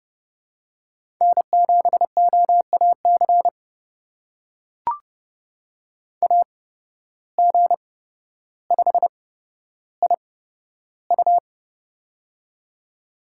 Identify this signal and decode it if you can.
Morse code, heard as a keyed tone, about 30 wpm, spaced as for 13 wpm, 700 Hz. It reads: N7OAC AG5IU
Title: USA Call Signs - 8x Character Spacing - 30wpm